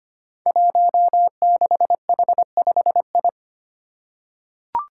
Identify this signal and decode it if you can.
Morse code, heard as a keyed tone, about 25 wpm, 700 Hz. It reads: 16H5I
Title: Five Challenging Letter-Number Combinations 25wpm